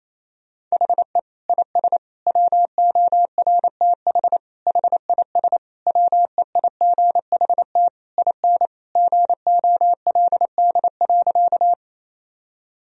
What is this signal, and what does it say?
Morse code, heard as a keyed tone, about 28 wpm, 700 Hz. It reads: HE IS WORTH HIS WEIGHT IN GOLD.
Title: Sentences from Top 1000 Words - Review Part 3 - 28wpm